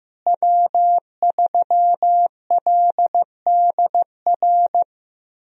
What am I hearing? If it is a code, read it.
Morse code, heard as a keyed tone, about 15 wpm, 700 Hz. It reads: W3LDR
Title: USA Call Signs - 5 Characters - Repeated 3x - 15wpm